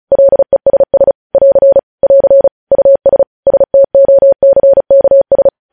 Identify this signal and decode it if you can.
Morse code, heard as a keyed tone, about 36 wpm, 550 Hz. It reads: LESS <AR> <AR> US STOCKS
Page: news headlines in morse code at 35 wpm